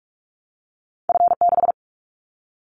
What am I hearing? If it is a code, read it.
Morse code, heard as a keyed tone, about 46 wpm, 690 Hz. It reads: F6
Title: Callsign Prefixes for Top 50 DXCC - 45wpm